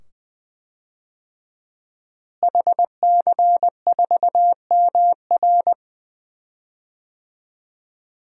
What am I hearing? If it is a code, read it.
Morse code, heard as a keyed tone, about 20 wpm, 700 Hz. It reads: HC4MR
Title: DXCC-100 Call Signs - 5 Characters - Repeated 2x - 20wpm